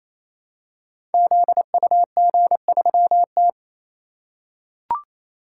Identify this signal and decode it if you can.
Morse code, heard as a keyed tone, about 27 wpm, 700 Hz. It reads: ZUG3T
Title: Code Groups - 5 Letter:Numbers - 28wpm